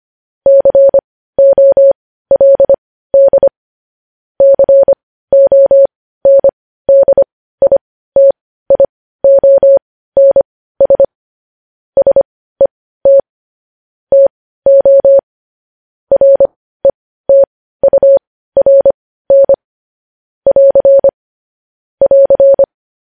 Morse code, heard as a keyed tone, about 25 wpm, spaced as for 15 wpm, 550 Hz. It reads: COLD CONDITIONS SET TO RETURN <AR> <AR>
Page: News Headlines in Morse Code at 15 WPM